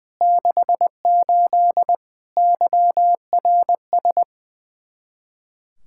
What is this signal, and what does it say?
Morse code, heard as a keyed tone, about 20 wpm, 700 Hz. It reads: 68 YRS
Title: Speed-Racing Common QSO Phrases - 20wpm